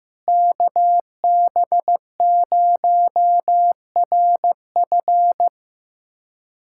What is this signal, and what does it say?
Morse code, heard as a keyed tone, about 15 wpm, 700 Hz. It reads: KB0RF
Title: USA Callsign DE Callsign - 5 Characters - 15wpm